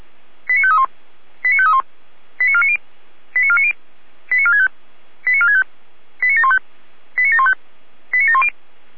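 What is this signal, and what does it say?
Five-tone selective call (ZVEI), heard as quick runs of five tones, tones of about 70 ms. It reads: 89531 89531 89490 89490 89465 89465 89715 89715 89710
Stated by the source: ZVEI five-tone call 89518 received by SDR, channel 455.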